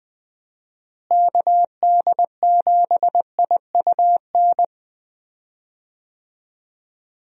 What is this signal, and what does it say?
Morse code, heard as a keyed tone, about 20 wpm, 700 Hz. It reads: KD7IUN
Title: USA Call Signs - 6 Characters - Repeated 3x - 20wpm